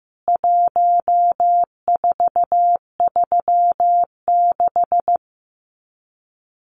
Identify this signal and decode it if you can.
Morse code, heard as a keyed tone, about 15 wpm, 700 Hz. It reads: 1436
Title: CWT Exchange 15wpm